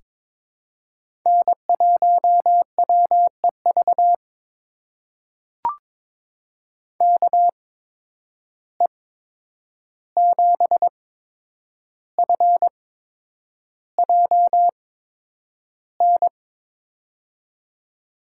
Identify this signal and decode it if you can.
Morse code, heard as a keyed tone, about 22 wpm, spaced as for 9 wpm, 700 Hz. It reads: N1WEV KE7FJN
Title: USA Call Signs - 8x Character Spacing - 22wpm